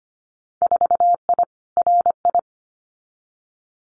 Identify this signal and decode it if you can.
Morse code, heard as a keyed tone, about 25 wpm, 700 Hz. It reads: 4I RI
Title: Winter Field Day Exchanges Encore 25wpm